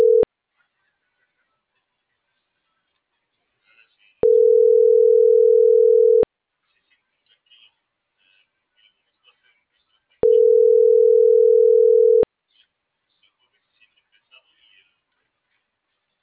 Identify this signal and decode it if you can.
Ringback tone (North America), 440 + 480 Hz.